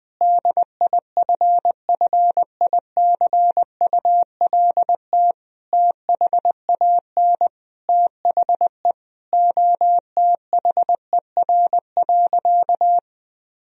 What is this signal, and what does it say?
Morse code, heard as a keyed tone, about 21 wpm, 700 Hz. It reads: DIFFICULT THAN THE OTHER.
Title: Sentences from Top 1000 Words - Review Part 2 - 20wpm